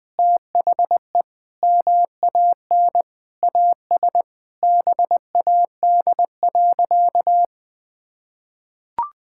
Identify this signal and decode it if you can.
Morse code, heard as a keyed tone, about 20 wpm, 700 Hz. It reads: THE MAN AS BAD.
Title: Sentences from Top 1000 Words - Review Part 6 - 20wpm